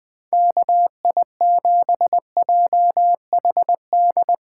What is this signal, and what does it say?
Morse code, heard as a keyed tone, about 20 wpm, 700 Hz. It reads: KI7JHD
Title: USA Callsign DE Callsign 20wpm